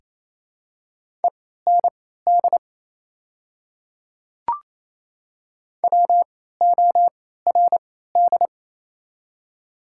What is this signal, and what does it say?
Morse code, heard as a keyed tone, about 28 wpm, spaced as for 16 wpm, 700 Hz. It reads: END WORD
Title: 3x Character Spacing - Top 200 Words - 28wpm